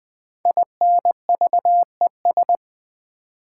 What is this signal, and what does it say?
Morse code, heard as a keyed tone, about 20 wpm, 700 Hz. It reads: INVES
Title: Build A Word - 5 Letters At A Time - 12-Letter Words - 20wpm